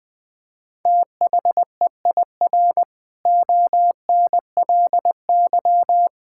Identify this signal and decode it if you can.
Morse code, heard as a keyed tone, about 20 wpm, 700 Hz. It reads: THEIR ONLY
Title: Sentences from Top 1600 Words 20wpm